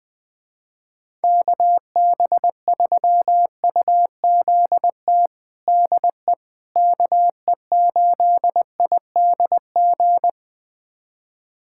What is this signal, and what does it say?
Morse code, heard as a keyed tone, about 21 wpm, 700 Hz. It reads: KB3UZT DE KE8IDG
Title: USA Callsign DE Callsign 20wpm